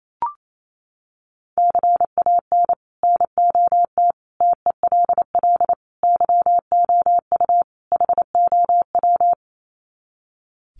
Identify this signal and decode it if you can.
Morse code, heard as a keyed tone, about 28 wpm, 700 Hz. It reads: CAN NOT TELL YOU HOW
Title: Sets of 5 Words Encore2 28wpm